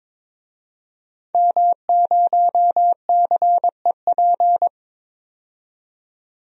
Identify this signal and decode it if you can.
Morse code, heard as a keyed tone, about 22 wpm, 700 Hz. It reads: M0CEP
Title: DXCC-100 Call Signs - 5 Characters - 22wpm